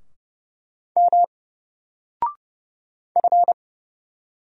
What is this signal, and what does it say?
Morse code, heard as a keyed tone, about 30 wpm, 700 Hz. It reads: M F
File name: Single-Letter-Number-30wpm